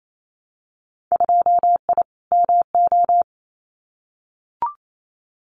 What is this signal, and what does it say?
Morse code, heard as a keyed tone, about 28 wpm, 700 Hz. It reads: 2I MO
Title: Winter Field Day Exchanges 28wpm